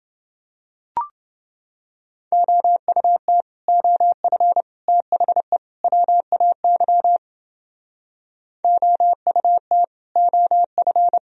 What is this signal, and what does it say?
Morse code, heard as a keyed tone, about 30 wpm, 700 Hz. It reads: OUT OF THE WAY OUT OF